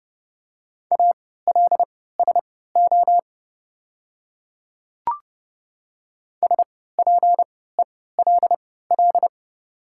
Morse code, heard as a keyed tone, about 30 wpm, spaced as for 17 wpm, 700 Hz. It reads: ALSO SPELL
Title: 3x Character Spacing - Top 200 Words - 30wpm